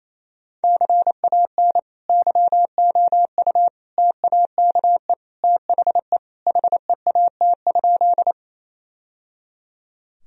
Morse code, heard as a keyed tone, about 28 wpm, 700 Hz. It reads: CAN YOU TAKE THE HEAT?
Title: Sentences from Top 1000 Words - Review Part 3 - 28wpm